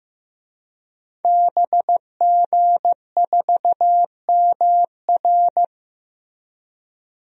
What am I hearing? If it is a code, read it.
Morse code, heard as a keyed tone, about 16 wpm, 700 Hz. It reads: BG4MR
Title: DXCC-100 Call Signs - 5 Characters - Repeated 2x - 15wpm